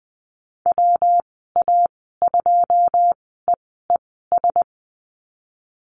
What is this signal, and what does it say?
Morse code, heard as a keyed tone, about 19 wpm, spaced as for 14 wpm, 700 Hz. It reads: WA2EES
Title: 140 Lesson 23g - Letter f - Call Signs 20wpm